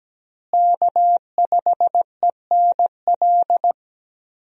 Morse code, heard as a keyed tone, about 17 wpm, 700 Hz. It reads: K5ENL
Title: USA Call Signs - 5 Characters - Repeated 3x - 17wpm